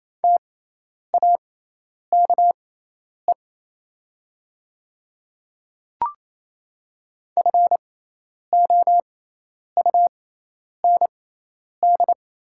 Morse code, heard as a keyed tone, about 27 wpm, spaced as for 10 wpm, 700 Hz. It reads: TAKE FOUND
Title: 6x Character Spacing - Top 300 Words - 28wpm